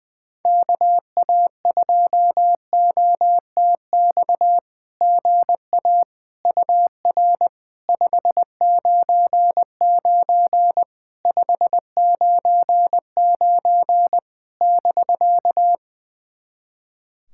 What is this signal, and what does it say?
Morse code, heard as a keyed tone, about 20 wpm, 690 Hz. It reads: KA2OTX GA UR 599 599 <BK>
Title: SOTA QSO Practice 20wpm